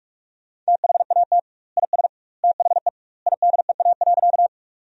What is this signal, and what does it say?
Morse code, heard as a keyed tone, about 43 wpm, 700 Hz. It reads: THAT IS THE IDEA.
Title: 2x Word-Spacing - Sentences from Top 300 Words - 45wpm